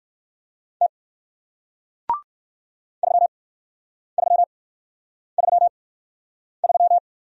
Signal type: Morse code

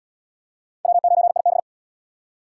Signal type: Morse code

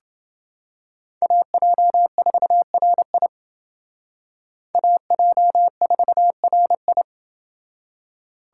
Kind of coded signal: Morse code

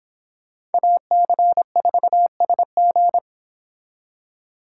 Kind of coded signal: Morse code

